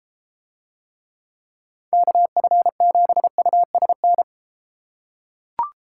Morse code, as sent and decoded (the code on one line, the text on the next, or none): -.- ..-. --... ..- ... -.
KF7USN